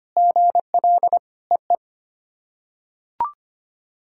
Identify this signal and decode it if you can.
Morse code, heard as a keyed tone, about 25 wpm, 700 Hz. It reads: GL EE
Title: SOTA QSO Practice 25wpm